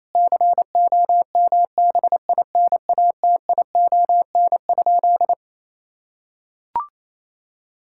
Morse code, heard as a keyed tone, about 28 wpm, 700 Hz. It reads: COMBINATION?
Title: Sentences from Top 1000 Words - Review Part 6 - 28wpm